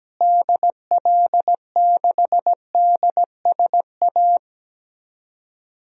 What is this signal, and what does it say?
Morse code, heard as a keyed tone, about 17 wpm, 700 Hz. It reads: DL6DSA
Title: DXCC-100 Call Signs - 6 Characters - Repeated 2x - 17wpm